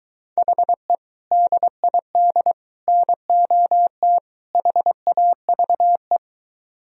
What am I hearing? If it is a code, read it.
Morse code, heard as a keyed tone, about 23 wpm, 700 Hz. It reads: HE DID NOT HAVE